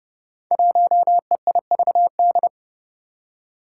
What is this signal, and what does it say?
Morse code, heard as a keyed tone, about 30 wpm, 700 Hz. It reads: 1EIVD
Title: Five Challenging Letter-Number Combinations 30wpm